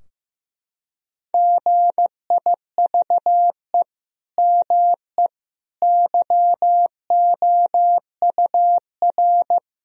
Morse code, heard as a keyed tone, about 15 wpm, 700 Hz. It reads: GIVE ME YOUR
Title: Sentences from Top 1000 Words - Review Part 4 - 15wpm